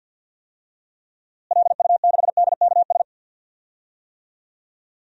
Morse code, heard as a keyed tone, about 52 wpm, 700 Hz. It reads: LU6DKS
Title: DXCC-100 Call Signs - 6 Characters - Repeated 3x - 50wpm